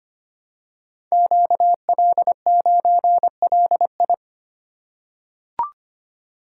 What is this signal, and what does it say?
Morse code, heard as a keyed tone, about 25 wpm, 700 Hz. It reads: QL9LI
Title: Code Groups - 5 Letter:Numbers - 25wpm